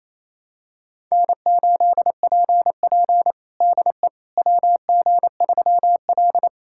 Morse code, heard as a keyed tone, about 28 wpm, 700 Hz. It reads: N8PP DE WG3L
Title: USA Callsign DE Callsign - 4 Characters - 28wpm